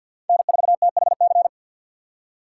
Morse code, heard as a keyed tone, about 53 wpm, 700 Hz. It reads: N4THC
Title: USA Call Signs - 5 Characters - 6x Character Spacing - 50wpm